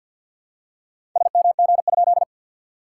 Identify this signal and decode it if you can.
Morse code, heard as a keyed tone, about 50 wpm, 690 Hz. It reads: SMG?
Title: SOTA QSO Practice 50wpm